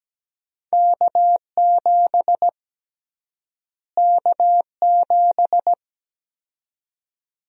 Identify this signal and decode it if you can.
Morse code, heard as a keyed tone, about 17 wpm, 700 Hz. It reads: K7 K7
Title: USA Call Sign Prefixes - Repeated 2x - 17wpm